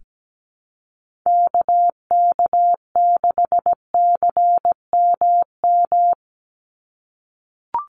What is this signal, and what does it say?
Morse code, heard as a keyed tone, about 17 wpm, 700 Hz. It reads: KK6CMM
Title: USA Call Signs 17wpm